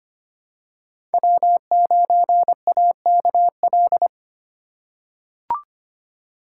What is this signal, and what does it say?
Morse code, heard as a keyed tone, about 25 wpm, 700 Hz. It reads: W9AKL
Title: USA Call Signs - 5 Characters - 4x Character Spacing - 25wpm